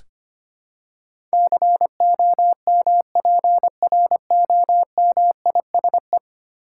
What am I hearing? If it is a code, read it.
Morse code, heard as a keyed tone, about 25 wpm, 700 Hz. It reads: COMPROMISE
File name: Ten Letter Words - 25wpm